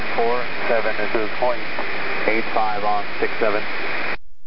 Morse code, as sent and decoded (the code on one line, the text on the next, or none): none